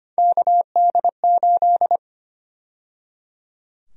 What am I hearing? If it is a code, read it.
Morse code, heard as a keyed tone, about 25 wpm, 700 Hz. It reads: KD8
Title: USA Call Sign Prefixes - 25wpm